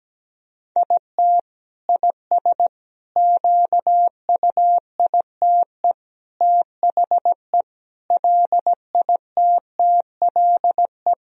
Morse code, heard as a keyed tone, about 17 wpm, 700 Hz. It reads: IT IS QUITE THE LITTLE
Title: Sentences from Top 1000 Words - Review Part 2 - 17wpm